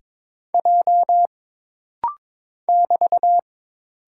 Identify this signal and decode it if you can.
Morse code, heard as a keyed tone, about 22 wpm, 700 Hz. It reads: J <BT>
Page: This practice set is composed of Letters, Numbers, common punctuations, and common prosigns.